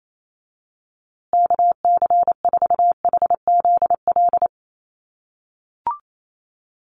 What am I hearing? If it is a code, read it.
Morse code, heard as a keyed tone, about 28 wpm, 700 Hz. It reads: KC4HZL
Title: USA Call Signs Repeated 3x 28wpm